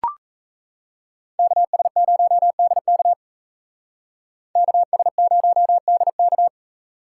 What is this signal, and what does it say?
Morse code, heard as a keyed tone, about 36 wpm, 700 Hz. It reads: KS0DK KS0DK